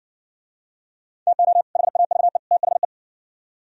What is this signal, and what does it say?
Morse code, heard as a keyed tone, about 59 wpm, 700 Hz. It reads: TO HAVE THE